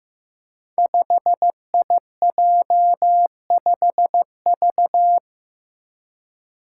Morse code, heard as a keyed tone, about 15 wpm, 700 Hz. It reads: 5IJ5V
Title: Five Challenging Letter-Number Combinations 15wpm